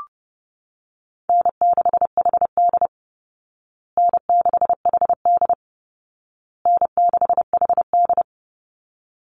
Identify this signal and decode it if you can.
Morse code, heard as a keyed tone, about 30 wpm, 700 Hz. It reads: N6HD N6HD N6HD
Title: Call Signs Repeated 2x - 30wpm